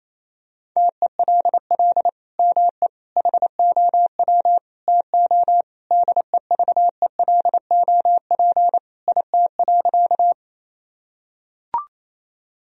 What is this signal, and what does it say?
Morse code, heard as a keyed tone, about 28 wpm, 700 Hz. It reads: TELL ME HOW TO DEVELOP IT.
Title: Sentences from Top 1000 Words - Review Part 2 - 28wpm